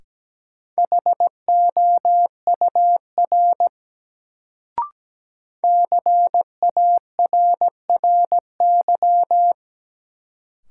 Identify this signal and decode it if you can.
Morse code, heard as a keyed tone, about 17 wpm, 700 Hz. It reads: HOUR CARRY
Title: Sentences from Top 400 Words - New Word Drill - 17wpm